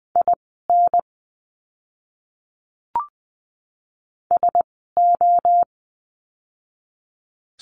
Morse code, heard as a keyed tone, about 19 wpm, spaced as for 14 wpm, 700 Hz. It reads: IN SO
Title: 230 Lesson 37f - Letter x - Words 20wpm